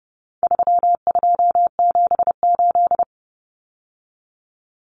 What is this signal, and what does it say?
Morse code, heard as a keyed tone, about 30 wpm, 700 Hz. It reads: 3278
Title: Common Cut Numbers - 4-Digit 30wpm